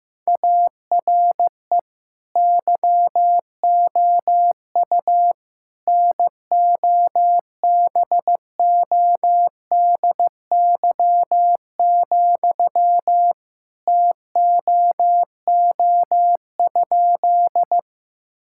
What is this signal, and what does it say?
Morse code, heard as a keyed tone, about 15 wpm, 700 Hz. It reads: ARE YOU NOBODY, TOO?
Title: Sentences from Top 1600 Words 15wpm